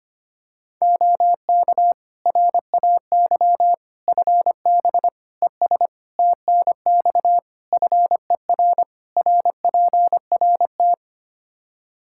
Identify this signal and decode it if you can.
Morse code, heard as a keyed tone, about 25 wpm, 700 Hz. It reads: OK RAY FB ES TNX FER RPRT
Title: Practice QSO - Exchange 2 - Info - 25wpm